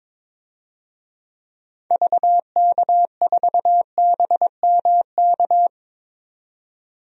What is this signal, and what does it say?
Morse code, heard as a keyed tone, about 22 wpm, 700 Hz. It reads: VK4BMK
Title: DXCC-100 Call Signs - 6 Characters - 22wpm